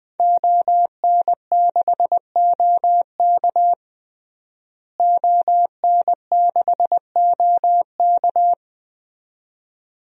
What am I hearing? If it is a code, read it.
Morse code, heard as a keyed tone, about 20 wpm, 700 Hz. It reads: ON6OK ON6OK
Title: DXCC-100 Call Signs - 5 Characters - Repeated 2x - 20wpm